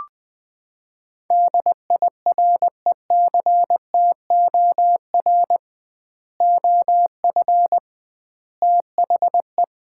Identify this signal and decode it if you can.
Morse code, heard as a keyed tone, about 20 wpm, 700 Hz. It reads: DIRECTOR OF THE
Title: 2x Word-Spacing - Sets of 3 Words - 20wpm